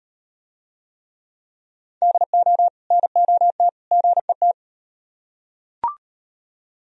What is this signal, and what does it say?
Morse code, heard as a keyed tone, about 35 wpm, 700 Hz. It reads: DO NOT GET